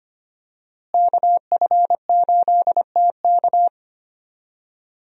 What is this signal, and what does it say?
Morse code, heard as a keyed tone, about 25 wpm, 700 Hz. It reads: KF8TK
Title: USA Call Signs - 5 Characters - Repeated 3x - 25wpm